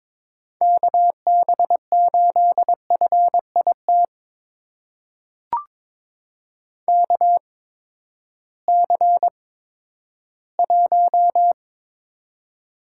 Morse code, heard as a keyed tone, about 22 wpm, spaced as for 12 wpm, 700 Hz. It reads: KB8FIT KC1
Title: USA Call Signs - 6 Characters - 8x Character Spacing - 22wpm